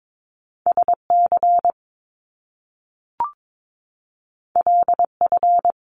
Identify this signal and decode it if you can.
Morse code, heard as a keyed tone, about 22 wpm, 700 Hz. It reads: SC LF
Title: Top Two Letter Combinations in Words 22wpm